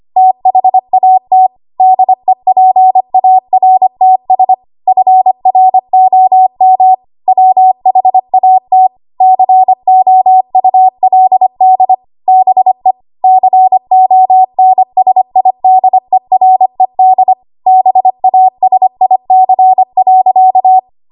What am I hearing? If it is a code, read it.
Morse code, heard as a keyed tone, about 25 wpm, 750 Hz. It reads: THAT DEPARTS FROM WHAT COULD BE CONSIDERED BASIC.
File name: Archive / 25 WPM / 25WPM